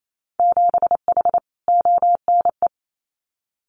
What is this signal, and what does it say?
Morse code, heard as a keyed tone, about 28 wpm, 700 Hz. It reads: 7H ONE
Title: Winter Field Day Exchanges Encore 28wpm